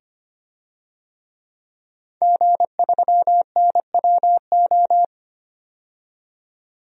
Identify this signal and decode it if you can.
Morse code, heard as a keyed tone, about 25 wpm, 700 Hz. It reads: G3NWO
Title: DXCC-100 Call Signs - 5 Characters - 25wpm